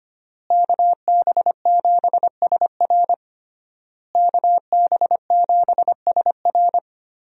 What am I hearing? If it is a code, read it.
Morse code, heard as a keyed tone, about 25 wpm, 700 Hz. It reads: KB7SR KB7SR
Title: USA Call Signs - 5 Characters - Repeated 3x - 25wpm